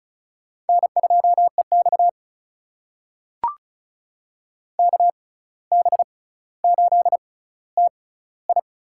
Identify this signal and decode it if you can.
Morse code, heard as a keyed tone, about 36 wpm, spaced as for 17 wpm, 700 Hz. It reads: N2EX KB8TI
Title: USA Call Signs - 6x Character Spacing - 35wpm